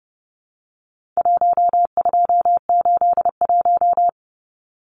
Morse code, heard as a keyed tone, about 30 wpm, 700 Hz. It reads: 1281
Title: Numbers - 4-Digit 30wpm